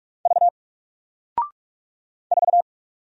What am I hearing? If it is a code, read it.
Morse code, heard as a keyed tone, about 46 wpm, 700 Hz. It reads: V 4
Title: Single Character (Letter:Number:Punctuation:Prosign) Alt 45wpm